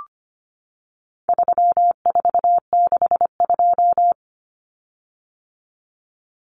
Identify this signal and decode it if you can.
Morse code, heard as a keyed tone, about 25 wpm, 700 Hz. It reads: 3462
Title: Common Cut Numbers - 4-Digit 25wpm